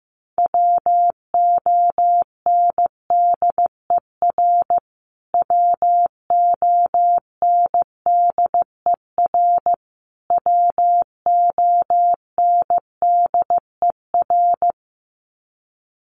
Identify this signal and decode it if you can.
Morse code, heard as a keyed tone, about 15 wpm, 700 Hz. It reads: WONDER WONDER WONDER
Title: Top 500 Words Repeated 15wpm